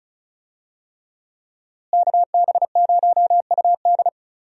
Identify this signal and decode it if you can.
Morse code, heard as a keyed tone, about 36 wpm, 700 Hz. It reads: KB0UD
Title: USA Call Signs - 5 Characters - Repeated 3x - 35wpm